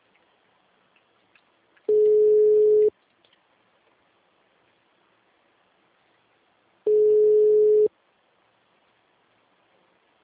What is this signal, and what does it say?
Ringback tone (Europe, 425 Hz).
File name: out- -2011- - - .58